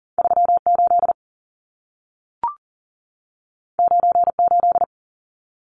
Morse code, heard as a keyed tone, about 40 wpm, 700 Hz. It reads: 38 98
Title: Numbers - 2-Digit 40wpm